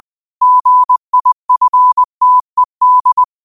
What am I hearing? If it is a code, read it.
Morse code, heard as a keyed tone, about 20 wpm, 1000 Hz. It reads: GIFTED
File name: qotdfiles / 20wpm